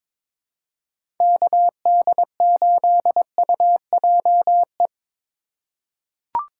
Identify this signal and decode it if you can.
Morse code, heard as a keyed tone, about 21 wpm, 700 Hz. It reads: KD8UJE
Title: USA Call Signs - 2x Character Spacing - 22wpm